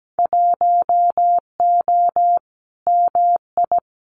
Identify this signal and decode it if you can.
Morse code, heard as a keyed tone, about 17 wpm, 700 Hz. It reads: 1O MI
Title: Winter Field Day Exchanges Encore 17wpm